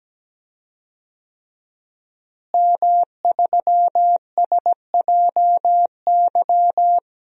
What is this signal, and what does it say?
Morse code, heard as a keyed tone, about 17 wpm, 700 Hz. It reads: M3SJY
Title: DXCC-100 Call Signs - 5 Characters - Repeated 3x - 17wpm